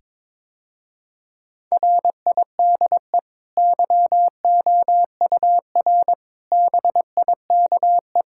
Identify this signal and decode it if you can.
Morse code, heard as a keyed tone, about 21 wpm, 700 Hz. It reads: RIDE YOUR BIKE
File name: practice / 22wpm / Easy Sentences with Hints - 22wpm